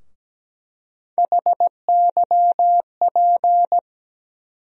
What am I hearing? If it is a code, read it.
Morse code, heard as a keyed tone, about 17 wpm, 700 Hz. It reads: HYP
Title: USA Call Sign Postfixes - 3 Letters 17wpm